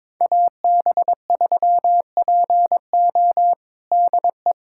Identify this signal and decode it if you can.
Morse code, heard as a keyed tone, about 22 wpm, 700 Hz. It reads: AB3PO DE
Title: USA Callsign DE Callsign - 5 Characters - 22wpm